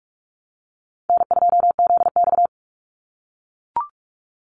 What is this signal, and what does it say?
Morse code, heard as a keyed tone, about 44 wpm, 700 Hz. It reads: N2ZX
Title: Call Signs - 45wpm